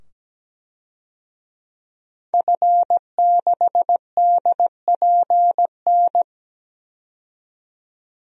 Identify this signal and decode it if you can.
Morse code, heard as a keyed tone, about 17 wpm, 700 Hz. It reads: F6DPN
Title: DXCC-100 Call Signs - 5 Characters - 17wpm